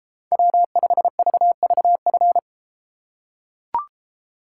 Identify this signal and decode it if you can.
Morse code, heard as a keyed tone, about 32 wpm, 700 Hz. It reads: W5VVF